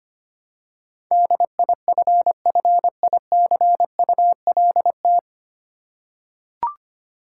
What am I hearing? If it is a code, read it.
Morse code, heard as a keyed tone, about 25 wpm, 700 Hz. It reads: DIFFICULT
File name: Nine Letter Words - 25wpm